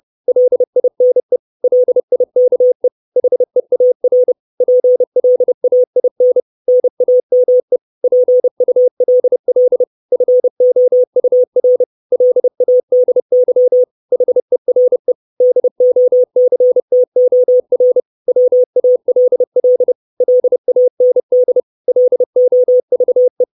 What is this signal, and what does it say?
Morse code, heard as a keyed tone, about 30 wpm, 500 Hz. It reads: LINE LIKE HEAR PLAIN NAME PULL FOUR LADY HERE DOCTOR WALL LAND LOVE
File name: 632 Words 30WPM